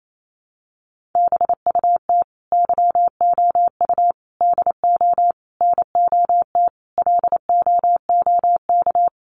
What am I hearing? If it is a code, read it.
Morse code, heard as a keyed tone, about 28 wpm, 700 Hz. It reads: BUT YOU DO NOT LOOK
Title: Sentences from Top 1100 Words 28wpm